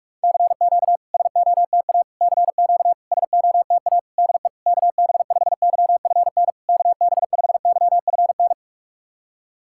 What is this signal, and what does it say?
Morse code, heard as a keyed tone, about 46 wpm, 690 Hz. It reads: CQ SOTA CQ SOTA DE KB5YFN KB5YFN
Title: SOTA QSO Practice 45wpm